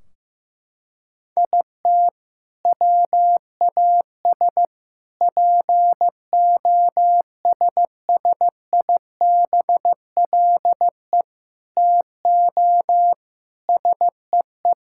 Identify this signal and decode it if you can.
Morse code, heard as a keyed tone, about 15 wpm, 700 Hz. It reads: IT WAS POSSIBLE TO SEE
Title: Sentences from Top 1600 Words 15wpm